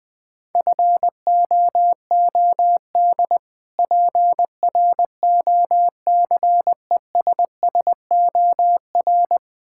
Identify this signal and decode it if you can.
Morse code, heard as a keyed tone, about 21 wpm, 700 Hz. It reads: FOOD PROCESSOR
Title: Difficult Words - Speed-Racing with Hints - 20wpm